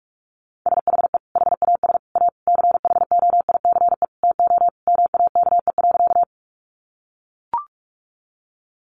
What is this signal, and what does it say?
Morse code, heard as a keyed tone, about 44 wpm, 700 Hz. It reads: SHE HAS A CHOICE TO MAKE.
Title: Sentences from Top 900 Words 45wpm